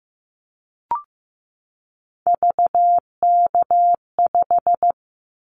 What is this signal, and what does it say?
Morse code, heard as a keyed tone, about 15 wpm, 700 Hz. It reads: VK5
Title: Callsign Prefixes for Top 100 DXCC - 15wpm